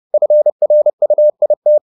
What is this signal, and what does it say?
Morse code, heard as a keyed tone, about 30 wpm, 600 Hz. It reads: FRUIT
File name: the old man and the sea / the old man and the sea 1 30wpm